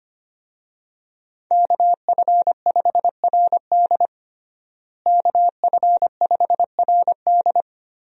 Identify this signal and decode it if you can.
Morse code, heard as a keyed tone, about 25 wpm, 700 Hz. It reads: KF5RD KF5RD
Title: USA Call Signs - 5 Characters - Repeated 2x - 25wpm